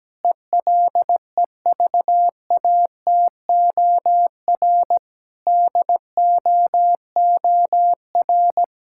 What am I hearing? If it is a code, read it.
Morse code, heard as a keyed tone, about 17 wpm, 700 Hz. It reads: ELEVATOR DOOR
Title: Sentences from Top 1600 Words 17wpm